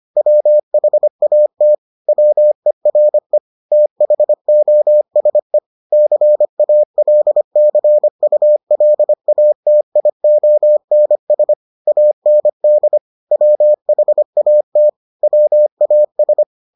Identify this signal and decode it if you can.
Morse code, heard as a keyed tone, about 25 wpm, 600 Hz. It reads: WHAT WERE THOSE CALCULATIONS AND WHAT WAS